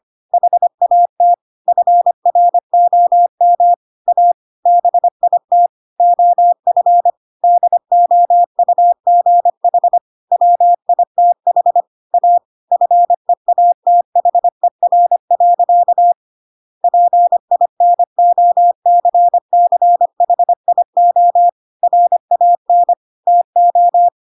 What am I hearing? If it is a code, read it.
Morse code, heard as a keyed tone, about 25 wpm, 700 Hz. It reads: HAT FROM A BIT OF DOUGH WITH A FEATHER. PINOCCHIO RAN TO